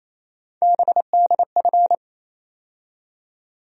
Morse code, heard as a keyed tone, about 27 wpm, 700 Hz. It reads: BDF
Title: USA Call Sign Postfixes - 3 Letters 28wpm